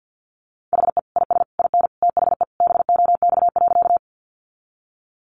Morse code, heard as a keyed tone, about 50 wpm, 700 Hz. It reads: HE IS IN THE BOX.
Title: Sentences from Top 500 Words 50wpm